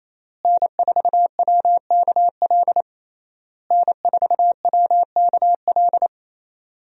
Morse code, heard as a keyed tone, about 28 wpm, 700 Hz. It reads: N4WKL N4WKL
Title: USA Call Signs - 5 Characters - Repeated 3x - 28wpm